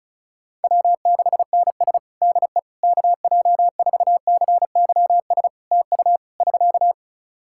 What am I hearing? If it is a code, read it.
Morse code, heard as a keyed tone, about 34 wpm, 700 Hz. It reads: W6NS DE KJ4CYS TU <SK>
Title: Practice QSO - Exchange 4 - Ending - 35wpm